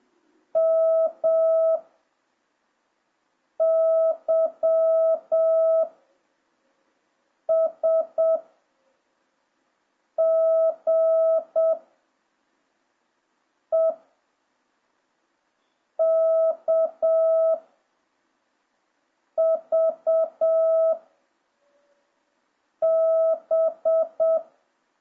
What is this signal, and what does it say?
Morse code, heard as a keyed tone, about 7 wpm, 650 Hz. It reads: M Y S G E K V B